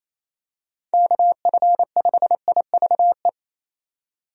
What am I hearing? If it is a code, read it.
Morse code, heard as a keyed tone, about 28 wpm, 700 Hz. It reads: KF5IVE
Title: USA Call Signs - 6 Characters - 28wpm